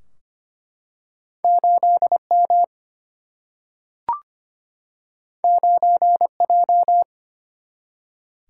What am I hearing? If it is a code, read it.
Morse code, heard as a keyed tone, about 25 wpm, 700 Hz. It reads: 8 M 9 J
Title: Single-Letter Single-Number 25wpm